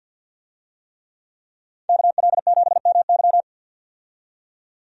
Morse code, heard as a keyed tone, about 48 wpm, 700 Hz. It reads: KL7MY
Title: USA Call Signs - 5 Characters - 50wpm